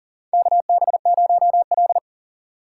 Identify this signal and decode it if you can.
Morse code, heard as a keyed tone, about 41 wpm, 700 Hz. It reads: KB0L
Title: USA Call Signs - 4 Characters - 2x Character Spacing - 40wpm